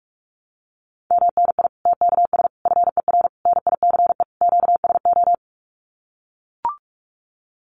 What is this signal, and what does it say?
Morse code, heard as a keyed tone, about 44 wpm, 700 Hz. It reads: MNI TKS FER NICE QSO
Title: Common QSO Phrases 45wpm